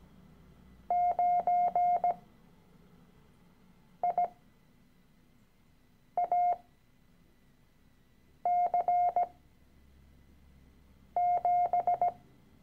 Morse code, heard as a keyed tone, about 17 wpm, spaced as for 4 wpm, 710 Hz. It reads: 9IAC7
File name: CW-Online-Kurs / Anfänger / Lektion 013